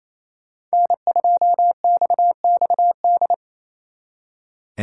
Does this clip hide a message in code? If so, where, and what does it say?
Yes: Morse code from 0.7 to 3.3 s, heard as a keyed tone, about 36 wpm, 700 Hz. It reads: N2XXD